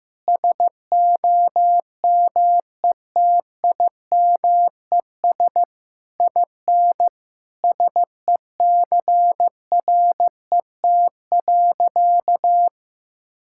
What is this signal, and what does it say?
Morse code, heard as a keyed tone, about 15 wpm, 700 Hz. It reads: SOMETIMES IN SECRET.
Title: Sentences from Top 1600 Words 15wpm